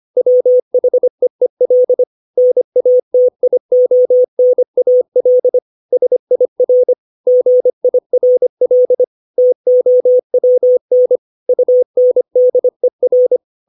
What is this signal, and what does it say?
Morse code, heard as a keyed tone, about 25 wpm, 500 Hz. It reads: WHEEL NATIONAL SIR GIRL TOWN UNDER
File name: 632 words / 632 Words 25WPM